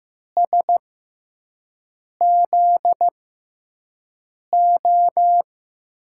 Morse code, heard as a keyed tone, about 15 wpm, spaced as for 5 wpm, 700 Hz. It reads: SZO
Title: USA Call Signs - 5 Characters - 6x Character Spacing - 15wpm